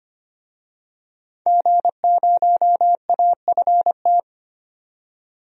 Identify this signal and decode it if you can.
Morse code, heard as a keyed tone, about 25 wpm, 700 Hz. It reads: G0AFT
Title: DXCC-100 Call Signs - 5 Characters - 25wpm